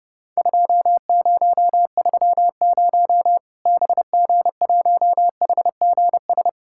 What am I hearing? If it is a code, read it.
Morse code, heard as a keyed tone, about 30 wpm, 700 Hz. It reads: 2030 BG1HGS
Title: Worked All Europe DX Contest - QTC Exchange - 30wpm